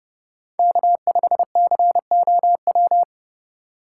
Morse code, heard as a keyed tone, about 31 wpm, 700 Hz. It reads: K5COW
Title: USA Call Signs - 5 Characters - 2x Character Spacing - 30wpm